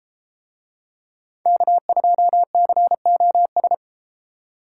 Morse code, heard as a keyed tone, about 32 wpm, 700 Hz. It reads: K2COS